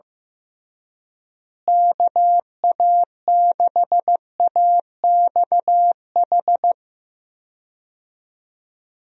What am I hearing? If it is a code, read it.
Morse code, heard as a keyed tone, about 15 wpm, 700 Hz. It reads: KA6AXH
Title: USA Call Signs - 6 Characters - Repeated 3x - 15wpm